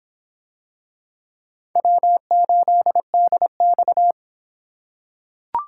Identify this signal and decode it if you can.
Morse code, heard as a keyed tone, about 26 wpm, 700 Hz. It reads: W8DX